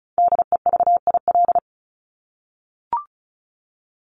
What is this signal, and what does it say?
Morse code, heard as a keyed tone, about 34 wpm, 700 Hz. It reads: DEVIL
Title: Top 1751-2000 Words 35wpm